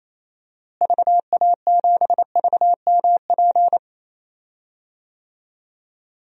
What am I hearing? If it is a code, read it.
Morse code, heard as a keyed tone, about 28 wpm, 700 Hz. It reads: VA7VMP
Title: DXCC-100 Call Signs - 6 Characters - Repeated 2x - 28wpm